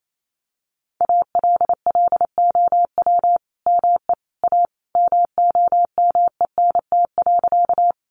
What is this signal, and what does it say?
Morse code, heard as a keyed tone, about 28 wpm, 700 Hz. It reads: ALLOW ME A MOMENT.
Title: Sentences from Top 1000 Words 28wpm